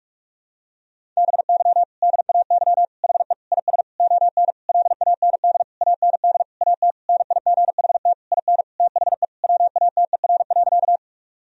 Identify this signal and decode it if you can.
Morse code, heard as a keyed tone, about 46 wpm, 700 Hz. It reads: BY DAY HE IS ON LAND AND AT NIGHT IN THE WATER.
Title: Sentences from Top 1000 Words - Review Part 1 - 45wpm